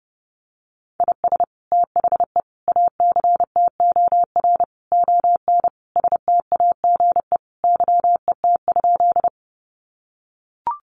Morse code, heard as a keyed tone, about 30 wpm, 700 Hz. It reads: IS THE ACTOR ON STAGE YET?
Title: Sentences from Top 1500 Words 30wpm